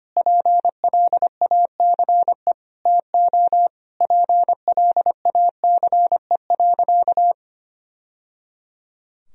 Morse code, heard as a keyed tone, about 25 wpm, 700 Hz. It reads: PLACE TO PLACE.
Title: Sentences from Top 1000 Words - Review Part 1 - 25wpm